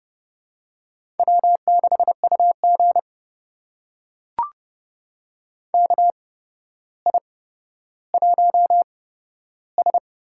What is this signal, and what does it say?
Morse code, heard as a keyed tone, about 30 wpm, spaced as for 13 wpm, 700 Hz. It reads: W6UG KI1S
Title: USA Call Signs - 4 Characters - 8x Character Spacing - 30wpm